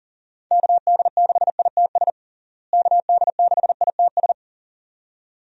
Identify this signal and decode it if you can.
Morse code, heard as a keyed tone, about 37 wpm, 700 Hz. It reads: KD6ITS KD6ITS
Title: USA Call Signs - 6 Characters - Repeated 2x - 40wpm